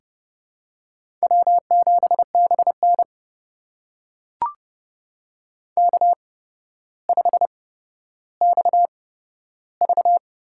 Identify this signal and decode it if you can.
Morse code, heard as a keyed tone, about 30 wpm, spaced as for 13 wpm, 700 Hz. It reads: W7BN K5XV
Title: USA Call Signs - 8x Character Spacing - 30wpm